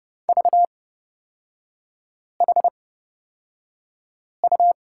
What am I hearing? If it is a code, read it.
Morse code, heard as a keyed tone, about 31 wpm, spaced as for 5 wpm, 690 Hz. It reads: VHU
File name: Lesson 37e - Letter x - Three Characters 30wpm